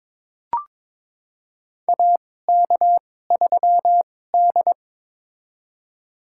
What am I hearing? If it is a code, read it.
Morse code, heard as a keyed tone, about 22 wpm, spaced as for 16 wpm, 700 Hz. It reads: AK3D
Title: USA Call Signs - 4 Characters - 2x Character Spacing - 22wpm